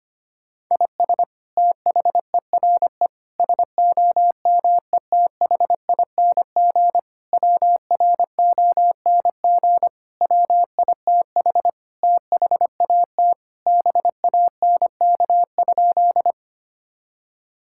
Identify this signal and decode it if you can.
Morse code, heard as a keyed tone, about 25 wpm, 700 Hz. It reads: IS THERE SOMETHING WRONG WITH THAT BANK?
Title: Sentences from Top 1000 Words - Review Part 6 - 25wpm